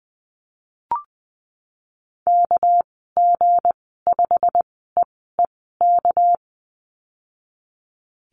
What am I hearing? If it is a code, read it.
Morse code, heard as a keyed tone, about 20 wpm, spaced as for 15 wpm, 700 Hz. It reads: KG5EEK
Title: 200 Lesson 32g - Letter k - Call Signs 20wpm